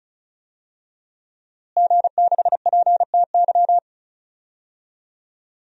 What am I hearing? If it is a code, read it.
Morse code, heard as a keyed tone, about 36 wpm, 700 Hz. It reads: G6PTY
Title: DXCC-100 Call Signs - 5 Characters - Repeated 2x - 35wpm